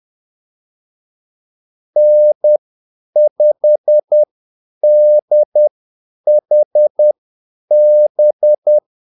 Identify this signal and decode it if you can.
Morse code, heard as a keyed tone, about 10 wpm, spaced as for 8 wpm, 600 Hz. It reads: N5DHB
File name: Callsigns-10-8wpm